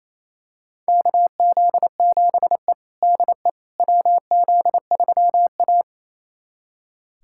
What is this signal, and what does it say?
Morse code, heard as a keyed tone, about 28 wpm, 700 Hz. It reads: KZ7E DE WZ3A
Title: USA Callsign DE Callsign - 4 Characters - 28wpm